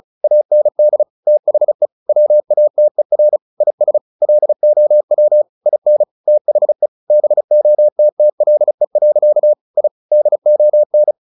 Morse code, heard as a keyed tone, about 34 wpm, 600 Hz. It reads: AND THE WATER IS LOW IN THE BOTTLE. I DON
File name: the old man and the sea / the old man and the sea 1 35wpm